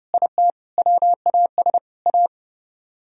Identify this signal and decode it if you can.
Morse code, heard as a keyed tone, about 38 wpm, 700 Hz. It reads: IT WAS A